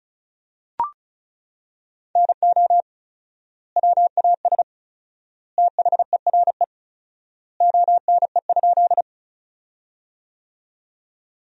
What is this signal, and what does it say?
Morse code, heard as a keyed tone, about 34 wpm, 700 Hz. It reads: NO WAS THERE ONE?
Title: 4x Word-Spacing - Sentences from Top 100 Words - 35wpm